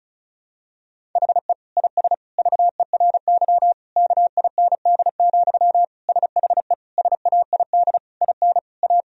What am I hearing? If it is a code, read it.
Morse code, heard as a keyed tone, about 36 wpm, 700 Hz. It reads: HE IS VERY KIND, SHE SAID IN A
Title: Sentences from Top 1000 Words - Review Part 1 - 35wpm